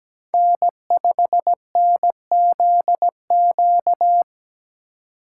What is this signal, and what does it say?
Morse code, heard as a keyed tone, about 17 wpm, 700 Hz. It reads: N5NZQ
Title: USA Call Signs - 5 Characters - 3x Character Spacing - 17wpm